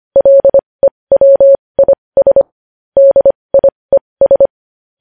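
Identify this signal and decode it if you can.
Morse code, heard as a keyed tone, about 25 wpm, spaced as for 20 wpm, 550 Hz. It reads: LEWIS DIES
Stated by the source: Morse code news headlines practice at 20 wpm.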